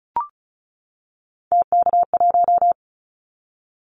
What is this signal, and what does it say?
Morse code, heard as a keyed tone, about 36 wpm, 700 Hz. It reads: TK1
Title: Callsign Prefixes for Top 100 DXCC - 35wpm